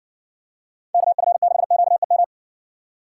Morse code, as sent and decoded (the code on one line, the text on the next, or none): -.- ..-. -.... --.- . --
KF6QEM